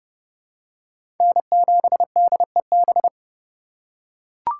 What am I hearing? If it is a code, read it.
Morse code, heard as a keyed tone, about 32 wpm, 700 Hz. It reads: N7DEB
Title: USA Call Signs - 5 Characters - 2x Character Spacing - 30wpm